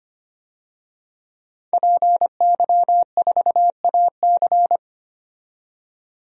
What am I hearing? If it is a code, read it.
Morse code, heard as a keyed tone, about 25 wpm, 700 Hz. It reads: PY4AC
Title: DXCC-100 Call Signs - 5 Characters - 25wpm